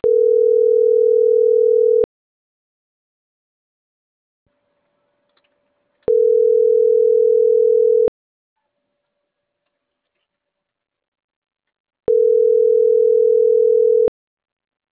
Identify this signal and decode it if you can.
Ringback tone (North America), 440 + 480 Hz.